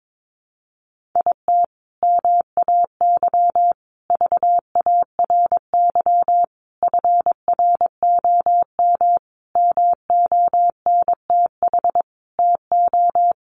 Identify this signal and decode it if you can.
Morse code, heard as a keyed tone, about 22 wpm, 700 Hz. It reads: IT MAY VARY FROM MONTH TO
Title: Sentences from Top 1200 Words 22wpm